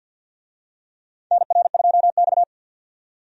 Morse code, heard as a keyed tone, about 47 wpm, 700 Hz. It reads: NR2X
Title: USA Call Signs - 4 Characters - 50wpm